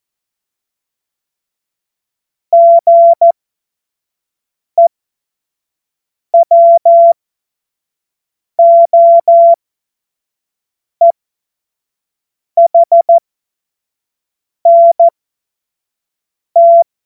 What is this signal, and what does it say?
Morse code, heard as a keyed tone, about 14 wpm, spaced as for 5 wpm, 680 Hz. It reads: GEWOEHNT
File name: Morsekurs / Witze / Witz 17. Januar 2023 25 F70